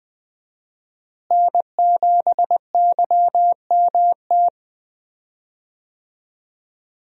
Morse code, heard as a keyed tone, about 20 wpm, 700 Hz. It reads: N7YMT
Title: USA Call Signs - 5 Characters - Repeated 2x - 20wpm